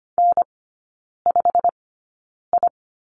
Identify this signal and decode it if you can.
Morse code, heard as a keyed tone, about 25 wpm, spaced as for 9 wpm, 700 Hz. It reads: N5I
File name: course / 25:10wpm / 093 Lesson 16e - Letter c - Three Characters 25wpm